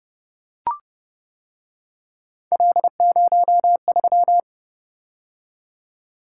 Morse code, heard as a keyed tone, about 29 wpm, 700 Hz. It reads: L03